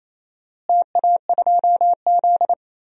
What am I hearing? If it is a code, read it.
Morse code, heard as a keyed tone, about 27 wpm, 700 Hz. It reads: TA2Z
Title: DXCC-100 Call Signs - 4 Characters - Repeated 3x - 28wpm